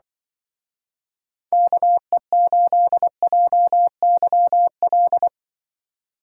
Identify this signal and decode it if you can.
Morse code, heard as a keyed tone, about 24 wpm, 700 Hz. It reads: KE8JYL
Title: Speed Racing - USA Call Signs 20wpm